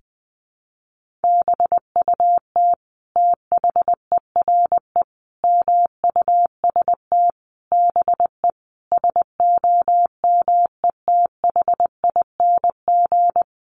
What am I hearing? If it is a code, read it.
Morse code, heard as a keyed tone, about 20 wpm, 700 Hz. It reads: BUT THERE MUST BE SOMETHING
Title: Sentences from Top 1500 Words 20wpm